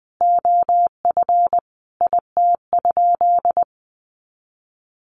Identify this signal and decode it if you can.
Morse code, heard as a keyed tone, about 19 wpm, 700 Hz. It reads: OF IT?
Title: Sentences from Top 300 Words 20wpm